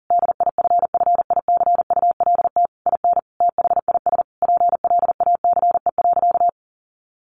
Morse code, heard as a keyed tone, about 40 wpm, 700 Hz. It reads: DIFFICULT IN THIS PLACE.
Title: Sentences from Top 1500 Words 40wpm